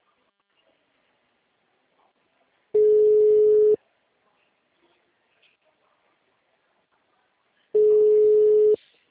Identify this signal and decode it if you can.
Ringback tone (Europe, 425 Hz).
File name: out- -2017